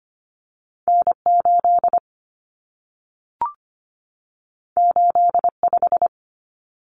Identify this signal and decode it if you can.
Morse code, heard as a keyed tone, about 25 wpm, 700 Hz. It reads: N8 85
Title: Common Cut Numbers - 2-Digit 25wpm